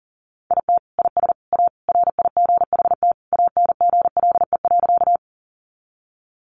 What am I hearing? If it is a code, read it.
Morse code, heard as a keyed tone, about 40 wpm, 700 Hz. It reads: IT IS A RIGHT ANGLE.
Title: Sentences from Top 1500 Words 40wpm